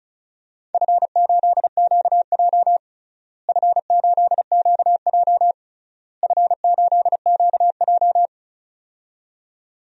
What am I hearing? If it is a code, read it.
Morse code, heard as a keyed tone, about 36 wpm, 700 Hz. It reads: F8QJ F8QJ F8QJ
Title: DXCC-100 Call Signs - 4 Characters - Repeated 3x - 35wpm